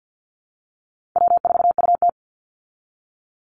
Morse code, heard as a keyed tone, about 48 wpm, 700 Hz. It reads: W4UT
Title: Easy Call Signs Repeated 2x - 50wpm